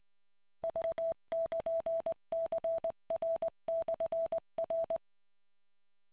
Morse code, heard as a keyed tone, about 20 wpm, 660 Hz. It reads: V<KN>CR/R